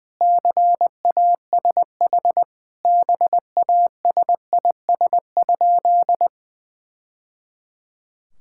Morse code, heard as a keyed tone, about 19 wpm, 700 Hz. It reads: CASH BASIS?
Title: Sentences from Top 1000 Words - Review Part 5 - 20wpm